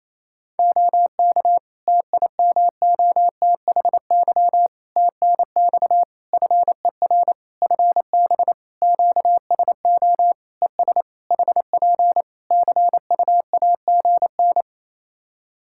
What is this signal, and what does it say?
Morse code, heard as a keyed tone, about 28 wpm, 700 Hz. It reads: OK TIMOTHY TNX FER FB QSO ES HP CUAGN
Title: K1USN SST QSO Practice 28wpm